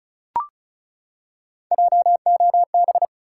Morse code, heard as a keyed tone, about 34 wpm, 700 Hz. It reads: JOB
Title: Three Letter Words - 35wpm